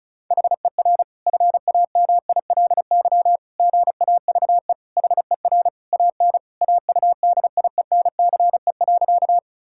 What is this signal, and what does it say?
Morse code, heard as a keyed tone, about 36 wpm, 700 Hz. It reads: HER FAMILY GAVE HER AN AUDIENCE.
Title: Sentences from Top 1000 Words - Review Part 5 - 35wpm